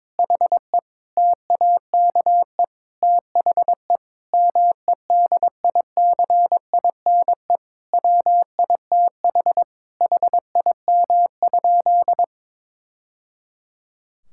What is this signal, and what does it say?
Morse code, heard as a keyed tone, about 22 wpm, 700 Hz. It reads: HE TAKE THE MEDICINE WITH HIM?
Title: Sentences from Top 1000 Words - Review Part 6 - 22wpm